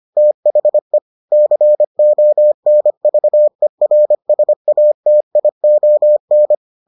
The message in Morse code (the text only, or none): THE CONVERSATION